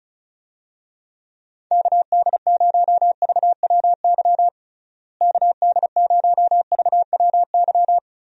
Morse code, heard as a keyed tone, about 36 wpm, 700 Hz. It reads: KD0VWY KD0VWY
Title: USA Call Signs - 6 Characters - Repeated 2x - 35wpm